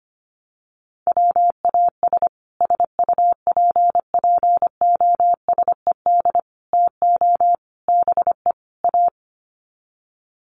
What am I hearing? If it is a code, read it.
Morse code, heard as a keyed tone, about 25 wpm, 700 Hz. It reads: WAS SUPPOSED TO BE A
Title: Sets of 5 Words Encore 25wpm